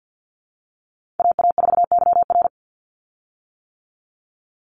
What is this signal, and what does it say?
Morse code, heard as a keyed tone, about 50 wpm, 700 Hz. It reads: AA4YR
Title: USA Call Signs 50wpm